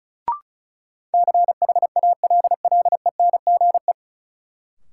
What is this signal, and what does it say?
Morse code, heard as a keyed tone, about 34 wpm, 700 Hz. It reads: CHALLENGE
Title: Sentences from Top 1000 Words - Review Part 7 - New Word Drill - 35wpm